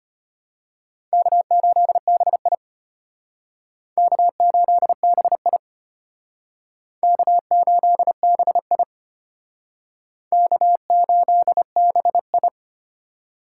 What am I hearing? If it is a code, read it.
Morse code, heard as a keyed tone, about 33 wpm, 700 Hz. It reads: K8BI K8BI K8BI K8BI